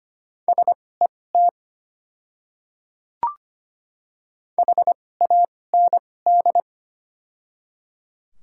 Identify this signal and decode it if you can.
Morse code, heard as a keyed tone, about 25 wpm, spaced as for 18 wpm, 700 Hz. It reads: SET HAND
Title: 2x Character Spacing - Top 200 Words - 25wpm